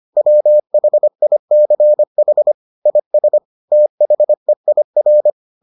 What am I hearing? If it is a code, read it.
Morse code, heard as a keyed tone, about 25 wpm, 600 Hz. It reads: WHICH IS THEIR